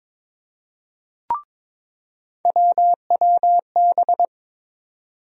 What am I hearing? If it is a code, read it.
Morse code, heard as a keyed tone, about 22 wpm, 700 Hz. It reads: WWB
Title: USA Call Sign Postfixes - 3 Letters 22wpm